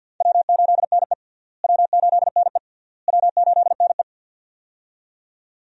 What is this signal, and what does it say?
Morse code, heard as a keyed tone, about 47 wpm, 700 Hz. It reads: W8NE W8NE W8NE
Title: USA Call Signs - 4 Characters - Repeated 3x - 50wpm